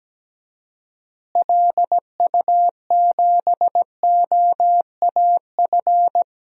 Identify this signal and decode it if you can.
Morse code, heard as a keyed tone, about 17 wpm, 700 Hz. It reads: LU7OAF
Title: DXCC-100 Call Signs - 6 Characters - Repeated 2x - 17wpm